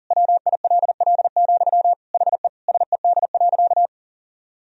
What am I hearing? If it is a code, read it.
Morse code, heard as a keyed tone, about 40 wpm, 700 Hz. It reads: WILL, HE SED.
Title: Sentences from Top 200 Words and Abbreviations 40wpm